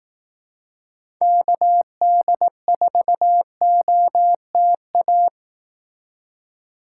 Morse code, heard as a keyed tone, about 18 wpm, 700 Hz. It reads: KD4OTA
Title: Speed Racing - USA 2x3 Call Signs 15wpm